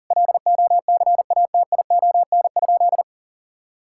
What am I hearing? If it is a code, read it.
Morse code, heard as a keyed tone, about 40 wpm, 700 Hz. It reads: LOCATION?
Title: Sentences from Top 1000 Words - Review Part 6 - 40wpm